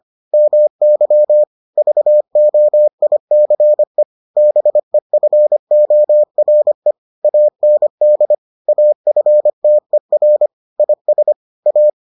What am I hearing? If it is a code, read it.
Morse code, heard as a keyed tone, about 25 wpm, 600 Hz. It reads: MY VOICE BEFORE AND AFTER IS A